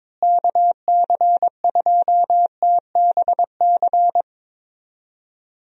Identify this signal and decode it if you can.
Morse code, heard as a keyed tone, about 22 wpm, 700 Hz. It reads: KC2TBC
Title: USA Call Signs - 6 Characters - Repeated 3x - 22wpm